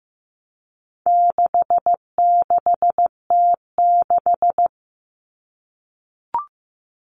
Morse code, heard as a keyed tone, about 15 wpm, 700 Hz. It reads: 66T6
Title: Common Cut Numbers - 4-Digit 15wpm